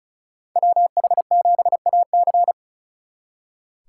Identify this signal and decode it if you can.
Morse code, heard as a keyed tone, about 46 wpm, 700 Hz. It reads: WH7AC